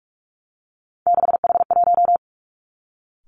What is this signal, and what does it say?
Morse code, heard as a keyed tone, about 46 wpm, 700 Hz. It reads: 6H1
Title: Three Challenging Letter-Number Combinations 45wpm